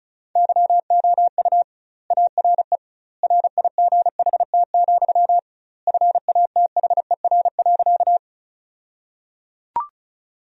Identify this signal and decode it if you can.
Morse code, heard as a keyed tone, about 36 wpm, 700 Hz. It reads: YOU ARE RIGHT, FATHER.
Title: 2x Word-Spacing - Sentences from Top 300 Words - 35wpm